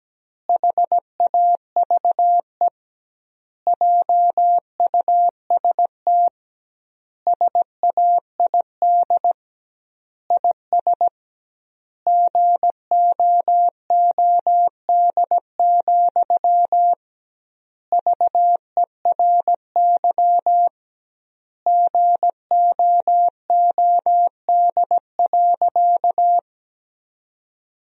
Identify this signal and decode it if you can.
Morse code, heard as a keyed tone, about 17 wpm, 700 Hz. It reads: HAVE JUST SAID IS GOOD, VERY GOOD.
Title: 2x Word-Spacing - Sentences from Top 200 Words - 17wpm